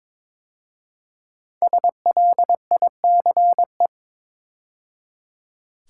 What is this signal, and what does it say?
Morse code, heard as a keyed tone, about 22 wpm, 690 Hz. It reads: SLICE
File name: 5 Letter Words - First 3 Letters Given - 22wpm